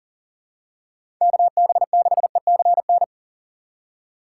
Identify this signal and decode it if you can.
Morse code, heard as a keyed tone, about 40 wpm, 700 Hz. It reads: KB6ECN
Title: USA Call Signs - 6 Characters - 6x Character Spacing - 40wpm